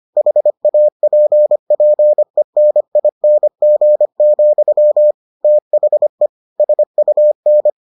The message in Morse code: HAPPENING, THE SUN

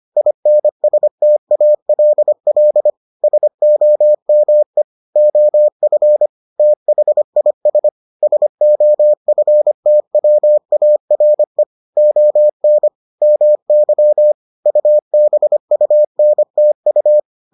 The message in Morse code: INSTALL SOME OF THIS SOFTWARE ON MY UBUNTU

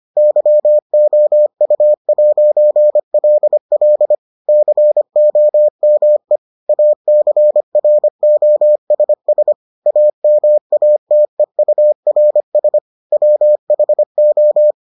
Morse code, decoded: YOU'LL COME ACROSS AMATEURS WHO

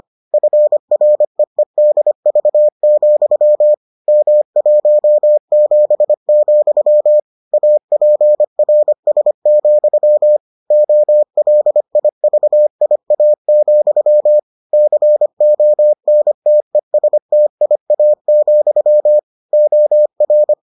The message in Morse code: FREEDV, M17, APRS, OLIVIA, CONTESTIA, OR

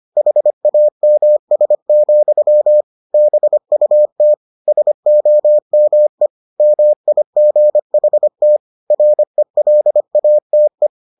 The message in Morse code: HAMS, BUT SOME MIGHT RELATE

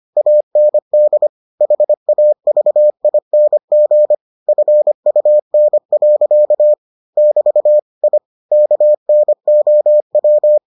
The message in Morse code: AND HAVING FUN. <BT> I KNOW

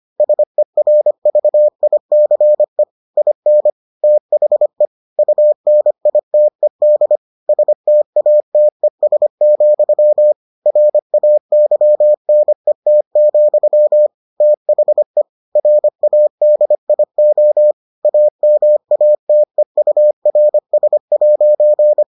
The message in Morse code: SERVICE IN THE UNITED STATES, RAYNET, THE RADIO AMATEURS'

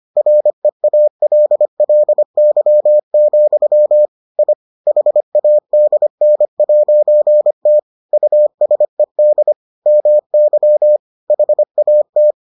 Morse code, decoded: REALLY, I HADN'T USED MY HAT

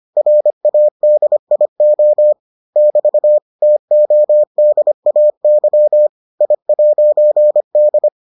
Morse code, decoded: RADIO <BT> TODAY I'D

